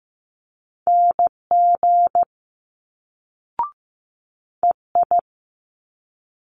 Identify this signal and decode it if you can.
Morse code, heard as a keyed tone, about 15 wpm, 700 Hz. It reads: NG EI
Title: Multi-letter Phonemes Quick Fire 15wpm